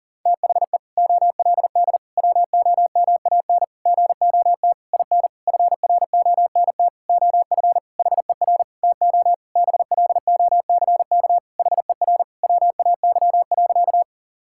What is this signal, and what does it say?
Morse code, heard as a keyed tone, about 40 wpm, 700 Hz. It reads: THE OLD WOMAN GOT IN FRONT OF HER TO BLOCK HER WAY.
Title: Sentences from Top 1600 Words 40wpm